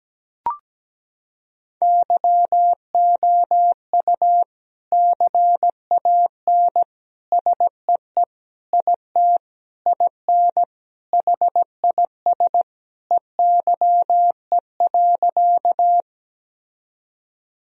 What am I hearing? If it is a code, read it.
Morse code, heard as a keyed tone, about 17 wpm, 700 Hz. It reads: YOU CAN SEE IT IN HIS EYE.
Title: Sentences from Top 1000 Words - Review Part 1 - 17wpm